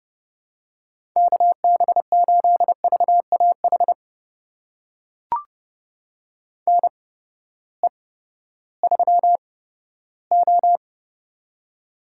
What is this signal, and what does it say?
Morse code, heard as a keyed tone, about 31 wpm, spaced as for 15 wpm, 700 Hz. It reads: KB8VAH NE3O
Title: USA Call Signs - 8x Character Spacing - 30wpm